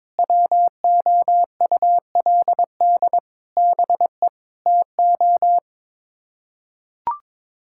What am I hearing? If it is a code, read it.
Morse code, heard as a keyed tone, about 22 wpm, 700 Hz. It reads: WOULD BE TO
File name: practice / 22wpm / Sets of 8 Words - 22wpm